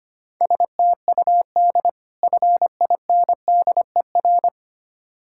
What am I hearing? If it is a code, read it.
Morse code, heard as a keyed tone, about 25 wpm, 700 Hz. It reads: STUD FINDER
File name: practice / 25wpm / Difficult Words with Hints - 25wpm